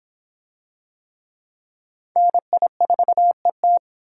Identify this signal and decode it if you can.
Morse code, heard as a keyed tone, about 22 wpm, 700 Hz. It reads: NI4ET